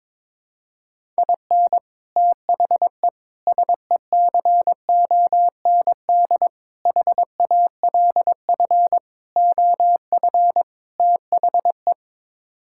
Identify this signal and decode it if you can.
Morse code, heard as a keyed tone, about 22 wpm, 700 Hz. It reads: IN THE SECOND HALF OF THE
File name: practice / 22wpm / Sets of 6 Words - 22wpm